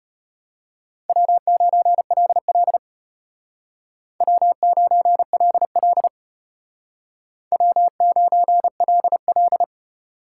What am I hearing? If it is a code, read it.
Morse code, heard as a keyed tone, about 32 wpm, 700 Hz. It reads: W9LL W9LL W9LL